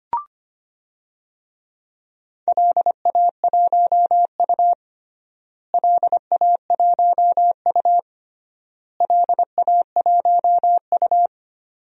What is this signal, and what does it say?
Morse code, heard as a keyed tone, about 25 wpm, 700 Hz. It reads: LA1U LA1U LA1U
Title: DXCC-100 Call Signs - 4 Characters - Repeated 3x - 25wpm